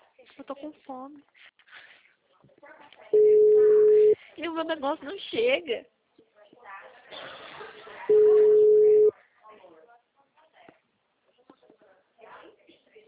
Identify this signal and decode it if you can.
Ringback tone (Europe, 425 Hz), with speech.